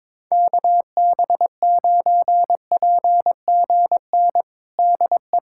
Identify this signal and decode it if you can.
Morse code, heard as a keyed tone, about 22 wpm, 700 Hz. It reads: KB9PGN DE
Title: USA Callsign DE Callsign - 6 Characters - 22wpm